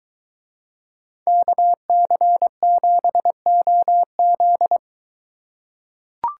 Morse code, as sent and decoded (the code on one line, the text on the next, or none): -.- -.-. --... --- --..
KC7OZ